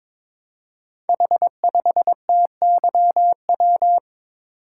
Morse code, heard as a keyed tone, about 22 wpm, 700 Hz. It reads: H5TYW
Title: Code Groups - 5 Letter:Numbers - 22wpm